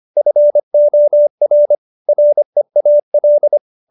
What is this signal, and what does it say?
Morse code, heard as a keyed tone, about 25 wpm, 590 Hz. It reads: FOR REAL